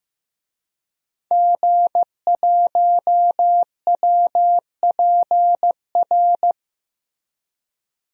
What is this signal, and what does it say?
Morse code, heard as a keyed tone, about 15 wpm, 700 Hz. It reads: G1WPR
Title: DXCC-100 Call Signs - 5 Characters - 15wpm